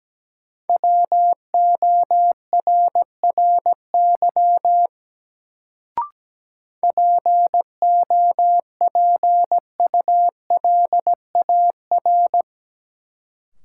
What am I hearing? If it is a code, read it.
Morse code, heard as a keyed tone, about 17 wpm, 700 Hz. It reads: WORRY POPULAR
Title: Sets of 5 Words - New Word Drill - 17wpm